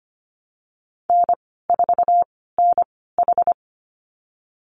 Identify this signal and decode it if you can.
Morse code, heard as a keyed tone, about 25 wpm, spaced as for 16 wpm, 700 Hz. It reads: N4NH
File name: Lesson 15g - Number 5 - Call Signs 25wpm